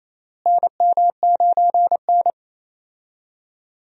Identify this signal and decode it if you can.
Morse code, heard as a keyed tone, about 27 wpm, 700 Hz. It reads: NM9N
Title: USA Call Signs - 4 Characters - Repeated 2x - 28wpm